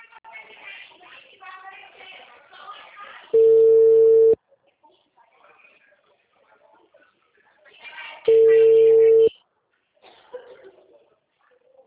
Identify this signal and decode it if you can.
Ringback tone (China, 450 Hz), with speech.